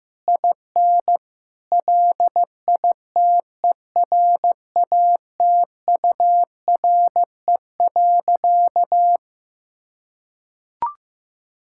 Morse code, heard as a keyed tone, about 15 wpm, 700 Hz. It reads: IN LITERATURE.
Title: Sentences from Top 1000 Words - Review Part 5 - 15wpm